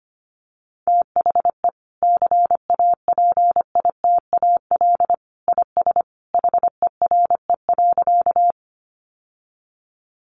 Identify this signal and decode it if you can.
Morse code, heard as a keyed tone, about 25 wpm, 700 Hz. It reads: THE CAPITAL IS HERE.
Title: Sentences from Top 1000 Words 25wpm